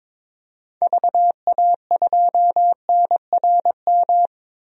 Morse code, heard as a keyed tone, about 22 wpm, 700 Hz. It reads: VA2NRM
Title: DXCC-100 Call Signs - 6 Characters - 22wpm